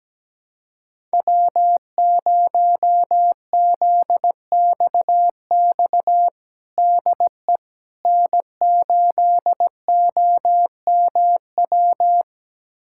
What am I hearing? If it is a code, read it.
Morse code, heard as a keyed tone, about 17 wpm, 700 Hz. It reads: W0ZXX DE N8OMW
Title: USA Callsign DE Callsign - 5 Characters - 17wpm